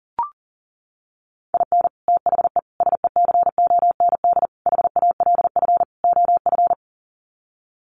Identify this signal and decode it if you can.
Morse code, heard as a keyed tone, about 40 wpm, 700 Hz. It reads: IN THE SECOND HALF OF
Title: Sets of 5 Words 40wpm